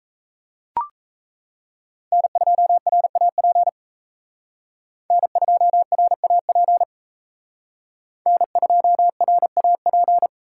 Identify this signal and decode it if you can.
Morse code, heard as a keyed tone, about 34 wpm, 700 Hz. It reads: N2RAP N2RAP N2RAP
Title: Speed Racing - USA Call Signs 28wpm